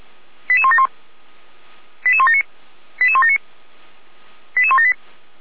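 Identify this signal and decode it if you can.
Five-tone selective call (ZVEI), heard as quick runs of five tones, tones of about 70 ms. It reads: 88171 88179 88179 88178
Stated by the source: Five-tone ZVEI selective call received on SDR.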